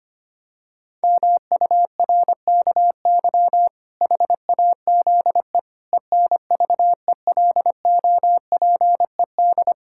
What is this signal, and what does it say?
Morse code, heard as a keyed tone, about 25 wpm, 700 Hz. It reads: MURKY HAZE ENVELOPED
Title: Pangram Challenge v2 - 25wpm